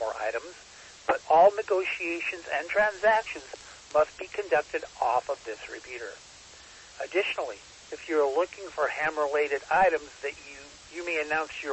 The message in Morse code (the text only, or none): none